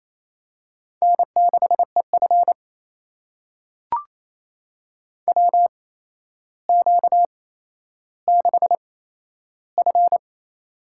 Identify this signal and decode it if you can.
Morse code, heard as a keyed tone, about 28 wpm, spaced as for 12 wpm, 700 Hz. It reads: N6EF WQ6F
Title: USA Call Signs - 4 Characters - 8x Character Spacing - 28wpm